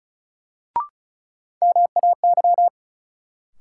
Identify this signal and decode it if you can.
Morse code, heard as a keyed tone, about 36 wpm, 700 Hz. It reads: MAY
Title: Sets of 2 Words - New Word Drill - 35wpm